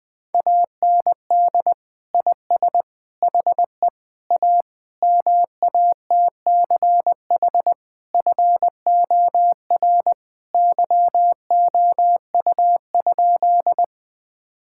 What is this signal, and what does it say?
Morse code, heard as a keyed tone, about 20 wpm, 700 Hz. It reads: AND IS HE A MATCH FOR YOU?
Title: Sentences from Top 1600 Words 20wpm